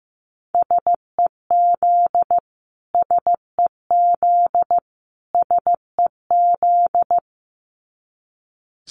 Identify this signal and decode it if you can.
Morse code, heard as a keyed tone, about 15 wpm, 700 Hz. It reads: SEZ SEZ SEZ
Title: Abbreviations Repeated 3x - 15wpm